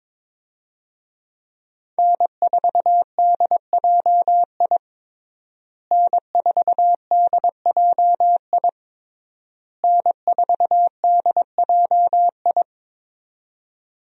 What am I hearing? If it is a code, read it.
Morse code, heard as a keyed tone, about 22 wpm, 700 Hz. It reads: N4DJI N4DJI N4DJI
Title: USA Call Signs - 5 Characters - Repeated 3x - 22wpm